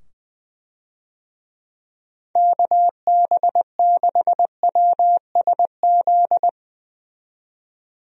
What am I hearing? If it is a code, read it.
Morse code, heard as a keyed tone, about 21 wpm, 700 Hz. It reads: KB6WSZ
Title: USA Call Signs - 6 Characters - 20wpm